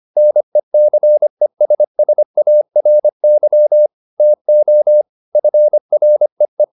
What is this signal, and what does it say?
Morse code, heard as a keyed tone, about 25 wpm, 600 Hz. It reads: NECESSARY TO FREE
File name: the old man and the sea part 1 25wpm